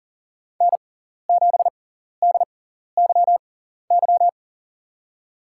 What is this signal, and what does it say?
Morse code, heard as a keyed tone, about 40 wpm, spaced as for 14 wpm, 700 Hz. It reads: N7DYY
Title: USA Call Signs - 5 Characters - 6x Character Spacing - 40wpm